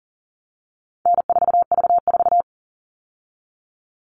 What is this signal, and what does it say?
Morse code, heard as a keyed tone, about 40 wpm, 700 Hz. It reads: N4V4
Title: All Cut Numbers - 4-Digit 40wpm